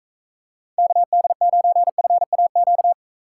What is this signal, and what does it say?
Morse code, heard as a keyed tone, about 45 wpm, 700 Hz. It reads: KD9FAQ